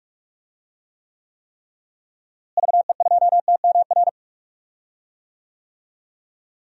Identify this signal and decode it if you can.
Morse code, heard as a keyed tone, about 46 wpm, 700 Hz. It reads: VE2TMR
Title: DXCC-100 Call Signs - 6 Characters - Repeated 3x - 45wpm